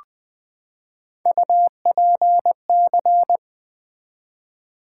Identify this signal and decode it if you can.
Morse code, heard as a keyed tone, about 20 wpm, 700 Hz. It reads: UPC
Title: Build A Word - 3 Letters At A Time - 8-Letter Words - 20wpm